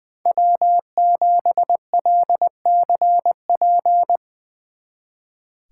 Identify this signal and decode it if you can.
Morse code, heard as a keyed tone, about 21 wpm, 700 Hz. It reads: W7LCP
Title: Speed Racing - USA 1x3 Call Signs 20wpm